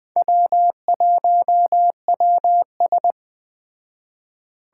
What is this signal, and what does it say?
Morse code, heard as a keyed tone, about 20 wpm, 700 Hz. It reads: W1WS
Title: Speed Racing - USA 1x2 Call Signs 17wpm